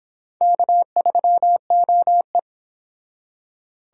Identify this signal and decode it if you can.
Morse code, heard as a keyed tone, about 25 wpm, 700 Hz. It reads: K3OE